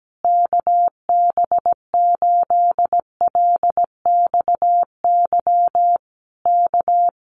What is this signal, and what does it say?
Morse code, heard as a keyed tone, about 17 wpm, 700 Hz. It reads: KB8LXY K
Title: Common QSO Phrases Encore 17wpm